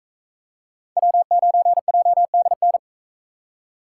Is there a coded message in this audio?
Yes: Morse code from 1.0 to 2.8 s, heard as a keyed tone, about 40 wpm, 700 Hz. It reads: W9JDN